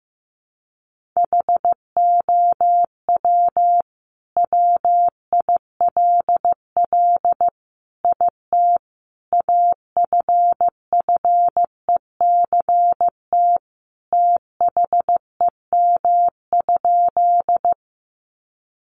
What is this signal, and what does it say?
Morse code, heard as a keyed tone, about 15 wpm, 700 Hz. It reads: HOW WILL IT AFFECT THEM?
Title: Sentences from Top 900 Words 15wpm